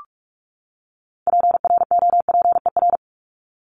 Morse code, heard as a keyed tone, about 44 wpm, 700 Hz. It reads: PROPER
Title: LY Suffix - Top 100 Words with Base Words 45wpm